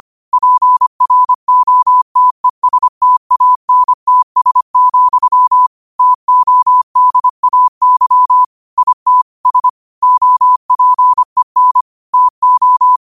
Morse code, heard as a keyed tone, about 25 wpm, 1000 Hz. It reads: PROTESTANTS, TODAY IT S OPEN TO
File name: qotdfiles / 25wpm